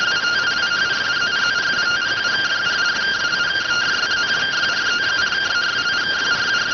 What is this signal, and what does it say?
Two-tone teleprinter signal, about 170 Hz shift.